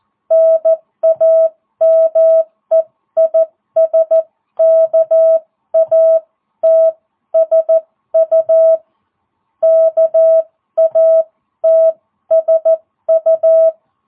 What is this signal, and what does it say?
Morse code, heard as a keyed tone, about 14 wpm, spaced as for 12 wpm, 660 Hz. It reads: NAME IS KATSU KATSU